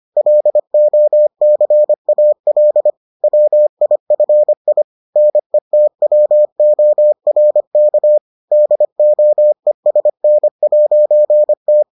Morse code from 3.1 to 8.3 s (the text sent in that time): WIFI NETWORK